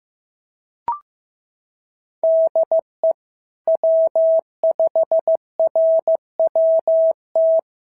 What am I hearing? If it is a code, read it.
Morse code, heard as a keyed tone, about 14 wpm, 650 Hz. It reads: DE W5RWT
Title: SOTA QSO Practice 15wpm